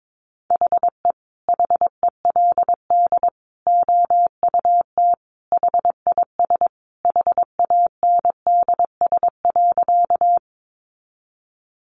Morse code, heard as a keyed tone, about 22 wpm, 700 Hz. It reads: HE HELD OUT HIS HANDS.
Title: A Sound Of Thunder 22wpm